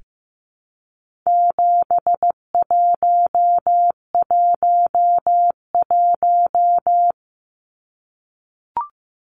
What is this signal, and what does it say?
Morse code, heard as a keyed tone, about 15 wpm, 700 Hz. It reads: 7111
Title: Numbers - 4-Digit 15wpm